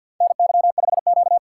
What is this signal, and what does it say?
Morse code, heard as a keyed tone, about 51 wpm, 700 Hz. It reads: NY5Q
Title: USA Call Signs - 4 Characters - 2x Character Spacing - 50wpm